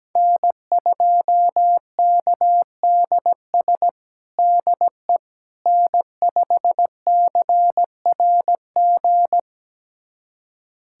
Morse code, heard as a keyed tone, about 17 wpm, 700 Hz. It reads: N2KDS DE N5CRG
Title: USA Callsign DE Callsign - 5 Characters - 17wpm